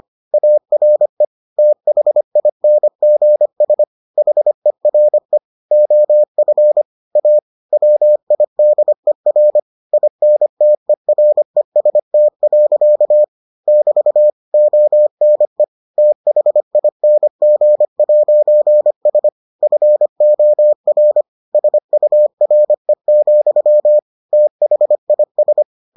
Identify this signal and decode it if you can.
Morse code, heard as a keyed tone, about 25 wpm, 600 Hz. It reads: ARE THINGS HERE OF A WIDER INTEREST. <BT> ONE THING'S FOR SURE, THIS